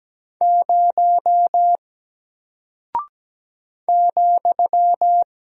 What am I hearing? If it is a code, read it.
Morse code, heard as a keyed tone, about 17 wpm, 700 Hz. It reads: 0 ,
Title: Single Character (Letter:Number:Punctuation:Prosign) Alt 17wpm